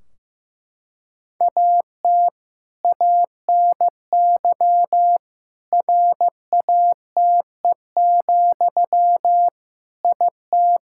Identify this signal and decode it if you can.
Morse code, heard as a keyed tone, about 15 wpm, 700 Hz. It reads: AT ANY RATE, IT
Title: Sentences from Top 1600 Words 15wpm